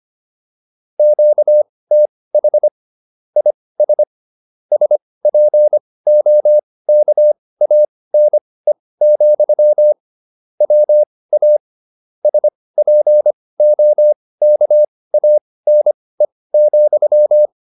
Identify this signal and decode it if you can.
Morse code, heard as a keyed tone, about 25 wpm, spaced as for 18 wpm, 600 Hz. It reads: QTH IS SPOKANE, WA SPOKANE,